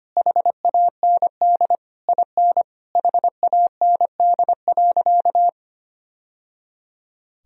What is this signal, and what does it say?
Morse code, heard as a keyed tone, about 25 wpm, 700 Hz. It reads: HAND IN HAND.
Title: Sentences from Top 1000 Words - Review Part 4 - 25wpm